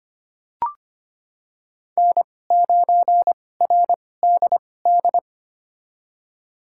Morse code, heard as a keyed tone, about 25 wpm, spaced as for 18 wpm, 700 Hz. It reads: N9RDD
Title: USA Call Signs - 5 Characters - 2x Character Spacing - 25wpm